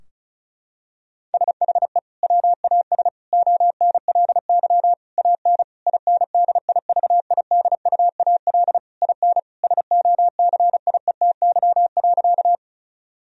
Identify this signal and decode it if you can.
Morse code, heard as a keyed tone, about 36 wpm, 700 Hz. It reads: SHE WAS ONLY AN INDIVIDUAL IN SOCIETY.
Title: Sentences from Top 1000 Words - Review Part 4 - 35wpm